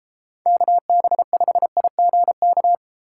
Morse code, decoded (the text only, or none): KB5IGK